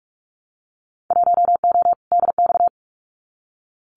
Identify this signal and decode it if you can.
Morse code, heard as a keyed tone, about 46 wpm, 700 Hz. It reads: 1O DX
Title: Winter Field Day Exchanges 45wpm